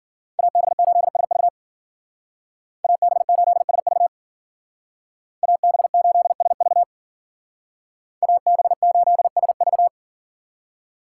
Morse code, heard as a keyed tone, about 48 wpm, 700 Hz. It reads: AB8SV AB8SV AB8SV AB8SV
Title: Speed Racing - USA 2x2 Call Signs 40wpm